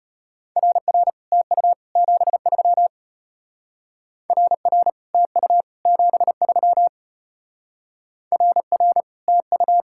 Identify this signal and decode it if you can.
Morse code, heard as a keyed tone, about 33 wpm, 700 Hz. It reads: RR TU 73 RR TU 73 RR TU